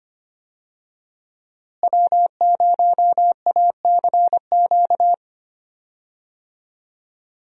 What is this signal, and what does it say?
Morse code, heard as a keyed tone, about 25 wpm, 700 Hz. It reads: W0ACQ
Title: USA Call Signs - 5 Characters - Repeated 3x - 25wpm